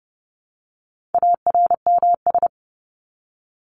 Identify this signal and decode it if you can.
Morse code, heard as a keyed tone, about 30 wpm, 700 Hz. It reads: ARMS
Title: ES:S Suffix - Top 100 Words with Base Words 30wpm